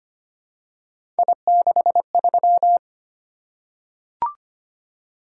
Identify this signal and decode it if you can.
Morse code, heard as a keyed tone, about 25 wpm, 700 Hz. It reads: I63
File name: Three Letter-Number Combinations - 25wpm